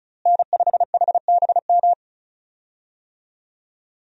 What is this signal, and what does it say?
Morse code, heard as a keyed tone, about 36 wpm, 700 Hz. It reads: N5HBM
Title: USA Call Signs - 5 Characters - Repeated 3x - 35wpm